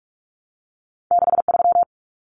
Morse code, heard as a keyed tone, about 46 wpm, 700 Hz. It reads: 63
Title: Numbers - 2-Digit 45wpm